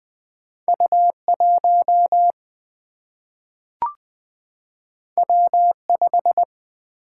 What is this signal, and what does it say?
Morse code, heard as a keyed tone, about 20 wpm, 700 Hz. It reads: U 1 W 5
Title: Single-Letter Single-Number 20wpm